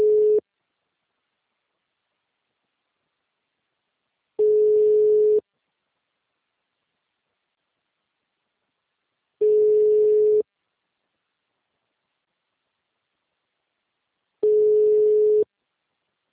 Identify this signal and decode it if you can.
Ringback tone (Europe, 425 Hz).